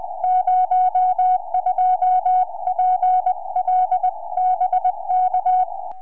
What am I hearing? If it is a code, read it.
Morse code, heard as a keyed tone, about 20 wpm, 740 Hz. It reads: 02PL BK
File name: oh6pa 4m